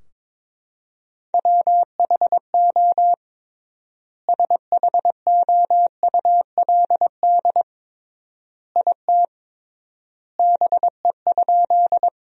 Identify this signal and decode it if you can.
Morse code, heard as a keyed tone, about 22 wpm, 700 Hz. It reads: WHO SHOULD IT BE?
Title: 3x Word-Spacing - Sentences from Top 300 Words - 22wpm